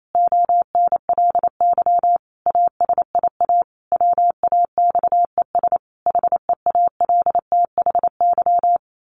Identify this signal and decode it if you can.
Morse code, heard as a keyed tone, about 28 wpm, 700 Hz. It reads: ONLY ASIA WAXES HEALTHY
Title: A Sound Of Thunder 28wpm